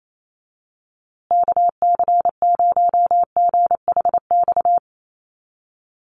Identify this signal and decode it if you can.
Morse code, heard as a keyed tone, about 28 wpm, 700 Hz. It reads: KC0GHX
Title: USA Call Signs Repeated 3x 28wpm